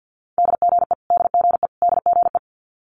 Morse code, heard as a keyed tone, about 50 wpm, 700 Hz. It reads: DGE DGE DGE
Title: Multi-letter Phonemes Repeated 3x 50wpm